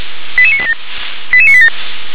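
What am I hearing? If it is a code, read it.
Five-tone selective call (ZVEI), heard as quick runs of five tones, tones of about 70 ms. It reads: 88987 88987
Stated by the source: Received on SDR; ZVEI five-tone call.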